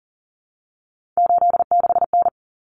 Morse code, heard as a keyed tone, about 40 wpm, 700 Hz. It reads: 86N
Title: Common Cut Numbers - 3-Digit 40wpm